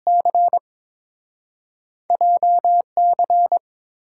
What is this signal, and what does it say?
Morse code, heard as a keyed tone, about 30 wpm, spaced as for 26 wpm, 700 Hz. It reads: C JC
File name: Speed-Racing - Two Letter-Number Combinations - 22wpm